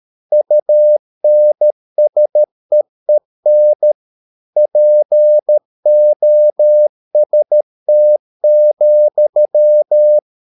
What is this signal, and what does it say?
Morse code, heard as a keyed tone, about 13 wpm, 600 Hz. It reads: UNSEEN POST,